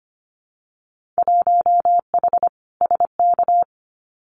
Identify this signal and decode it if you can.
Morse code, heard as a keyed tone, about 25 wpm, 700 Hz. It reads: 1H SK
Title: Winter Field Day Exchanges 25wpm